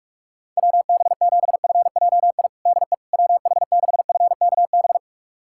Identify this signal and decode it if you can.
Morse code, heard as a keyed tone, about 41 wpm, 700 Hz. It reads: WB7FJI DE WH6FKB
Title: USA Callsign DE Callsign - 6 Characters - 45wpm